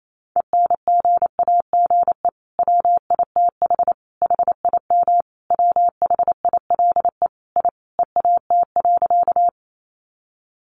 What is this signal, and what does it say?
Morse code, heard as a keyed tone, about 28 wpm, 700 Hz. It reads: ENGAGE WITH HIM WHILE I EAT.
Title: Sentences from Top 1500 Words 28wpm